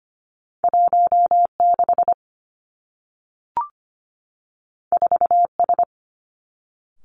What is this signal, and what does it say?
Morse code, heard as a keyed tone, about 25 wpm, 700 Hz. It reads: 16 4S
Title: Two Challenging Letter-Number Combinations 25wpm